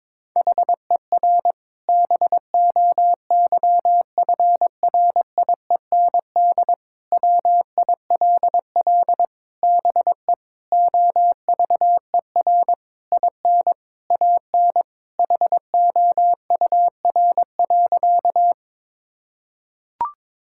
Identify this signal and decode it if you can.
Morse code, heard as a keyed tone, about 22 wpm, 700 Hz. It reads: HER BOYFRIEND WILL BE OVER IN AN HOUR.
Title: Sentences from Top 1000 Words - Review Part 5 - 22wpm